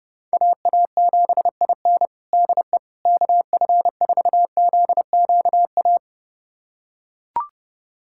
Morse code, heard as a keyed tone, about 30 wpm, 700 Hz. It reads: AA7IN DE KF4ZQA
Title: USA Callsign DE Callsign 30wpm